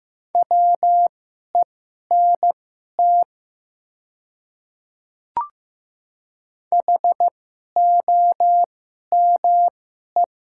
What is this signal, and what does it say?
Morse code, heard as a keyed tone, about 15 wpm, spaced as for 11 wpm, 700 Hz. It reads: WENT HOME
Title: 2x Character Spacing - Top 300 Words - 15wpm